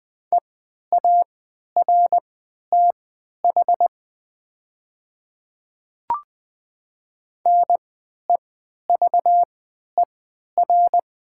Morse code, heard as a keyed tone, about 20 wpm, spaced as for 11 wpm, 700 Hz. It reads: EARTH NEVER
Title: 3x Character Spacing - Top 300 Words - 20wpm